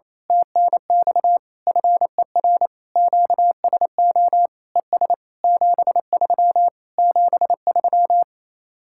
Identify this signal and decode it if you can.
Morse code, heard as a keyed tone, about 26 wpm, 700 Hz. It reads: TNX FER QSO ES 73 73
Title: Speed-Racing Common QSO Phrases - 28wpm